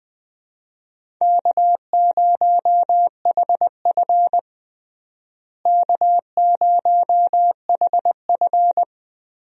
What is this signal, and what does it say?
Morse code, heard as a keyed tone, about 20 wpm, 700 Hz. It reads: K0HF K0HF
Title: USA Call Signs - 4 Characters - Repeated 2x - 20wpm